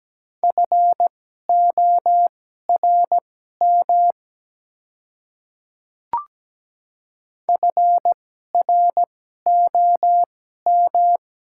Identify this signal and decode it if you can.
Morse code, heard as a keyed tone, about 17 wpm, spaced as for 12 wpm, 700 Hz. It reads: FORM FROM
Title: 2x Character Spacing - Top 200 Words - 17wpm